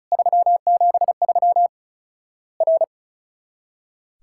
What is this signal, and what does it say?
Morse code, heard as a keyed tone, about 36 wpm, 700 Hz. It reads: 373 R
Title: Worked All Europe DX Contest - QTC Exchange - 35wpm